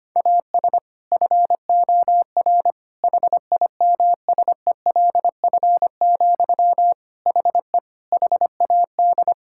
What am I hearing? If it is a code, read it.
Morse code, heard as a keyed tone, about 25 wpm, 700 Hz. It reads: AS FOR HIMSELF, HE HAD
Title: Sentences from Top 1600 Words 25wpm